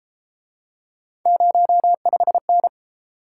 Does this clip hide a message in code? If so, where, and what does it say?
Yes: Morse code from 1.3 to 2.7 s, heard as a keyed tone, about 33 wpm, 700 Hz. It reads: 05N